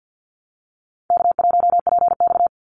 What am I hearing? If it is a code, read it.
Morse code, heard as a keyed tone, about 50 wpm, 700 Hz. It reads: K1PX
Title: Call Signs - 50wpm